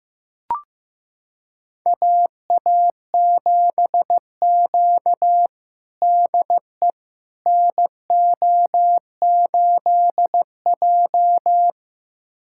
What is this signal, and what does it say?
Morse code, heard as a keyed tone, about 15 wpm, 700 Hz. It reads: AA7Q DE NO8J
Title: USA Callsign DE Callsign - 4 Characters - 15wpm